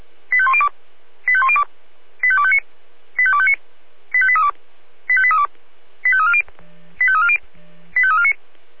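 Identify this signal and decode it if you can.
Five-tone selective call (ZVEI), heard as quick runs of five tones, tones of about 70 ms. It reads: 85102 85102 85259 85259 85832 85832 85340 85340 85349
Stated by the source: ZVEI five-tone selective call 85201 received by SDR.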